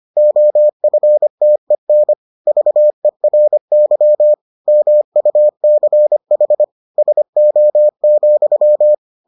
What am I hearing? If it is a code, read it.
Morse code, heard as a keyed tone, about 25 wpm, 600 Hz. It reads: OFTEN VERY MUCH SO,